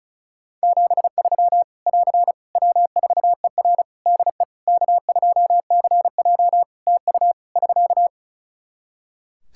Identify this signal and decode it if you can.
Morse code, heard as a keyed tone, about 34 wpm, 700 Hz. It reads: 73 <AR> W4ER DE K2CJ TU <SK>
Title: Practice QSO - Exchange 4 - Ending - 35wpm